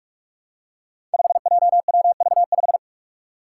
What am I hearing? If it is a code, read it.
Morse code, heard as a keyed tone, about 42 wpm, 700 Hz. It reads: 5JWV5
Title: Five Challenging Letter-Number Combinations 45wpm